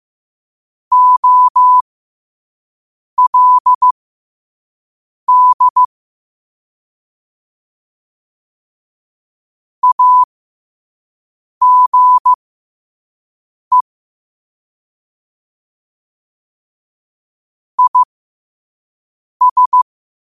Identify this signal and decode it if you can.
Morse code, heard as a keyed tone, about 15 wpm, spaced as for 5 wpm, 1000 Hz. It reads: OLD AGE IS
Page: qotd-at-5-wpm